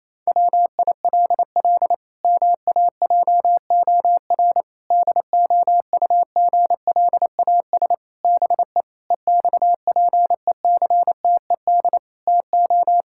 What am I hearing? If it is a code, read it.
Morse code, heard as a keyed tone, about 28 wpm, 700 Hz. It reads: WILL MAJOR DOUGLAS BE EXPECTED TO
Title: Pangram Challenge v2 - 28wpm